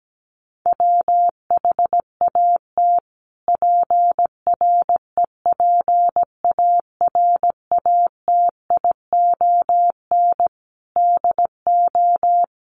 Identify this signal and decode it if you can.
Morse code, heard as a keyed tone, about 17 wpm, 700 Hz. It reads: WHAT PREPARATION DO
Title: Sentences from Top 1400 Words 17wpm